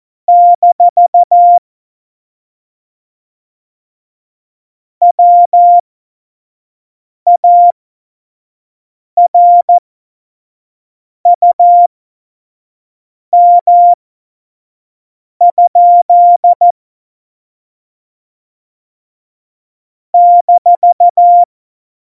Morse code, heard as a keyed tone, about 14 wpm, spaced as for 5 wpm, 700 Hz. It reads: - WARUM? -
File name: Morsekurs / Witze / Witz 06. November 2018, 25 F70